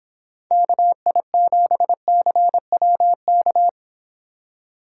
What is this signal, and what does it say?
Morse code, heard as a keyed tone, about 29 wpm, 700 Hz. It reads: KI7CWK